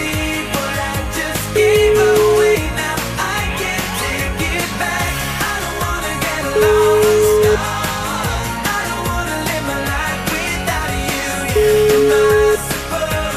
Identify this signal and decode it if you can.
Ringback tone (Europe, 425 Hz), with noise.